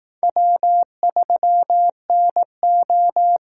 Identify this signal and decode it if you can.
Morse code, heard as a keyed tone, about 18 wpm, 700 Hz. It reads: W3NO
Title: Speed Racing - USA 1x2 Call Signs 15wpm